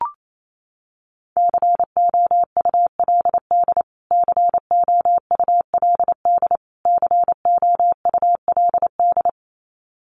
Morse code, heard as a keyed tone, about 28 wpm, 700 Hz. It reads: COULD COULD COULD
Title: Top 200 Words Repeated 28wpm